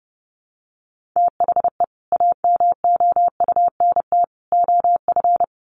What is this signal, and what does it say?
Morse code, heard as a keyed tone, about 30 wpm, 700 Hz. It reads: THE AMOUNT OF
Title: Sets of 3 Words 30wpm